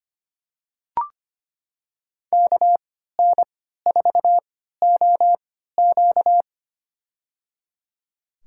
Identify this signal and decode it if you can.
Morse code, heard as a keyed tone, about 25 wpm, spaced as for 14 wpm, 700 Hz. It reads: KN4OQ
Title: USA Call Signs - 5 Characters - 3x Character Spacing - 25wpm